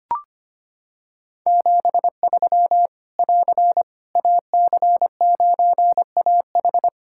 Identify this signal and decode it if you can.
Morse code, heard as a keyed tone, about 25 wpm, 700 Hz. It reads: 73 <AR> AC9AH
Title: K1USN SST QSO Practice 25wpm